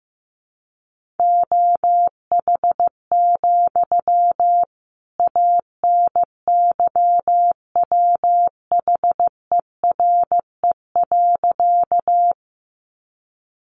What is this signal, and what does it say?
Morse code, heard as a keyed tone, about 15 wpm, 700 Hz. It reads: OH, ANYWHERE.
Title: Sentences from Top 1400 Words 15wpm